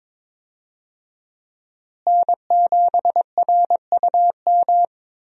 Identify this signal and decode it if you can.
Morse code, heard as a keyed tone, about 22 wpm, 700 Hz. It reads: N7RUM
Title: USA Call Signs - 5 Characters - Repeated 2x - 22wpm